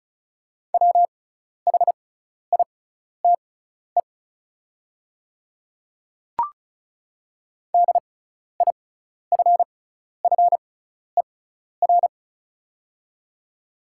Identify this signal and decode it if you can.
Morse code, heard as a keyed tone, about 36 wpm, spaced as for 12 wpm, 690 Hz. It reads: WHITE DIFFER
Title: 6x Character Spacing - Top 300 Words - 35wpm